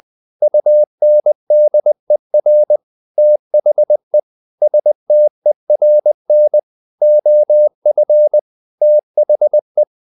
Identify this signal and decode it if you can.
Morse code, heard as a keyed tone, about 20 wpm, 600 Hz. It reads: UNDER THE STERN OF THE
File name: the old man and the sea / the old man and the sea part 1 20wpm